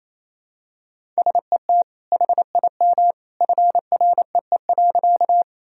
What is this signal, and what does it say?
Morse code, heard as a keyed tone, about 28 wpm, 700 Hz. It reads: SET HIM FREE.
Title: Sentences from Top 1000 Words - Review Part 1 - 28wpm